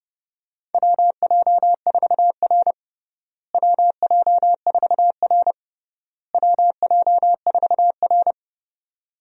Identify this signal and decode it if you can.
Morse code, heard as a keyed tone, about 30 wpm, 700 Hz. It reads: WJ4R WJ4R WJ4R
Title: USA Call Signs - 4 Characters - Repeated 3x - 30wpm